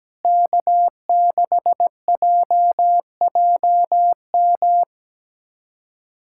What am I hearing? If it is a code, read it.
Morse code, heard as a keyed tone, about 17 wpm, 700 Hz. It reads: K6JJM
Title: USA Callsign DE Callsign - 5 Characters - 17wpm